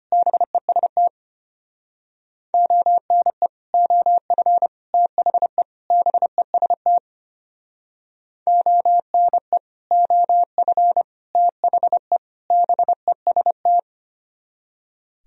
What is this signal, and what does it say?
Morse code, heard as a keyed tone, about 31 wpm, 700 Hz. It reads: BEST ONE OF THE BEST ONE OF THE BEST